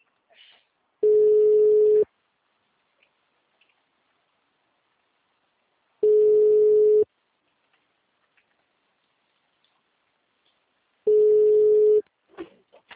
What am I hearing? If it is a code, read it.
Ringback tone (Europe, 425 Hz).
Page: Ribeirao sala 906